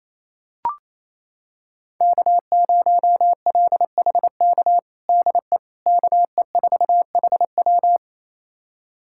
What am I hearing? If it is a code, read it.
Morse code, heard as a keyed tone, about 28 wpm, 700 Hz. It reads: K0LHK DE KE4HW
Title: USA Callsign DE Callsign - 5 Characters - 28wpm